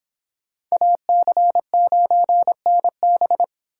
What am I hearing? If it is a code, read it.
Morse code, heard as a keyed tone, about 26 wpm, 700 Hz. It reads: AC9NB